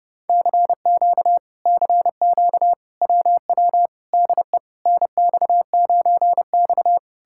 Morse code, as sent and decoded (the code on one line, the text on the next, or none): -.-. --.- / -.-. --.- / .-- .-- / -.. / -. -..- ----. -..-
CQ CQ WW DE NX9X